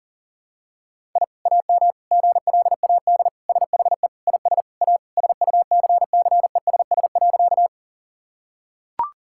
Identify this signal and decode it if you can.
Morse code, heard as a keyed tone, about 40 wpm, 700 Hz. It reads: I AM GLAD SHE IS A SUCCESS.
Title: Sentences from Top 1000 Words - Review Part 6 - 40wpm